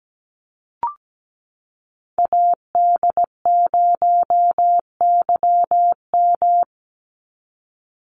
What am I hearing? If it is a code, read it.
Morse code, heard as a keyed tone, about 17 wpm, 700 Hz. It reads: AD0YM
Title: NA SOTA Chaser Callsigns 17wpm